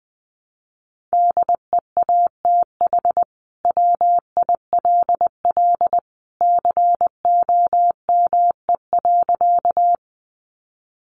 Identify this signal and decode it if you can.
Morse code, heard as a keyed tone, about 20 wpm, 700 Hz. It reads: DEATH WILL COME.
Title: Sentences from Top 1100 Words 20wpm